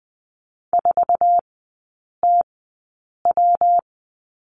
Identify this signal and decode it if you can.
Morse code, heard as a keyed tone, about 20 wpm, spaced as for 8 wpm, 700 Hz. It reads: 4TW
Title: 114 Lesson 19e - Letter w - Three Characters 20wpm